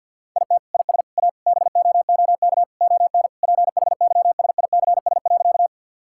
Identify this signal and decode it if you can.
Morse code, heard as a keyed tone, about 52 wpm, 700 Hz. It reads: IT IS A BOOK ON PHYSICS.
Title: Sentences from Top 1000 Words - Review Part 5 - 50wpm